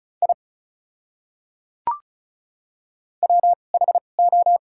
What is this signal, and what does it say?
Morse code, heard as a keyed tone, about 34 wpm, spaced as for 25 wpm, 700 Hz. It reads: I WHO
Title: 2x Character Spacing - Top 200 Words - 35wpm